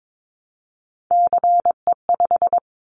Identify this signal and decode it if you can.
Morse code, heard as a keyed tone, about 22 wpm, 700 Hz. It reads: CE5
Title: Callsign Prefixes for Top 100 DXCC - 22wpm